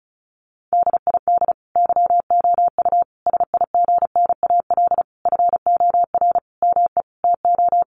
Morse code, heard as a keyed tone, about 34 wpm, 700 Hz. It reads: DID YOU SIGNAL FOR ME TO
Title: Sentences from Top 1400 Words 35wpm